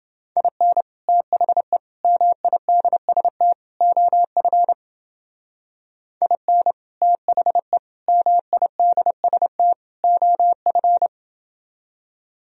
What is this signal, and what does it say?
Morse code, heard as a keyed tone, about 28 wpm, 700 Hz. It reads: IN THE MIDST OF IN THE MIDST OF